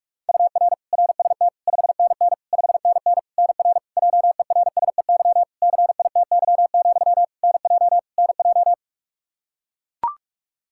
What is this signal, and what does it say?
Morse code, heard as a keyed tone, about 41 wpm, 700 Hz. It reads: UR RST 5NN 5NN NR JERSEY CITY, NJ NJ
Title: Practice QSO - Exchange 1 - Beginning - 45wpm